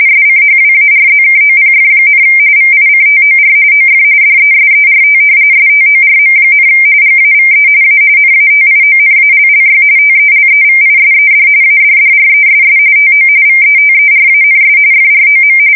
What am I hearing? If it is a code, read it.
RTTY, about 45.45 baud. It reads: SCHWARZKOPF, VICTOR OF DESERT STORM, GAVE A NATIONALLY TELEVISED ADDRESS TO VETERANS FROM
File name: fsk170 / battleshipnj